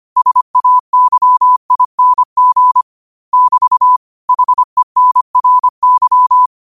Morse code, heard as a keyed tone, about 25 wpm, 1000 Hz. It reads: SAYING <BT> HENRY
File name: qotdfiles / 25wpm